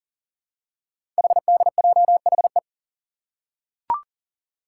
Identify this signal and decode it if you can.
Morse code, heard as a keyed tone, about 41 wpm, 700 Hz. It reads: HDJHE
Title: Code Groups - 5 Letters - 40wpm